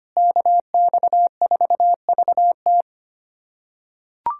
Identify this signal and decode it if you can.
Morse code, heard as a keyed tone, about 25 wpm, 700 Hz. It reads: KX4VT
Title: USA Callsign DE Callsign 25wpm